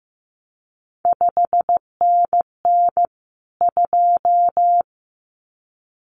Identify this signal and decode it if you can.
Morse code, heard as a keyed tone, about 15 wpm, 700 Hz. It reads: 5NN 2
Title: CQ WW DX Contest Exchange - 15wpm